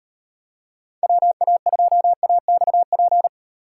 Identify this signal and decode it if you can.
Morse code, heard as a keyed tone, about 38 wpm, 700 Hz. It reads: WA2AXP